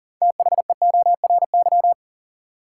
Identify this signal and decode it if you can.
Morse code, heard as a keyed tone, about 40 wpm, 700 Hz. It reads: THEORY
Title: Sentences from Top 600 Words - New Word Drill - 40wpm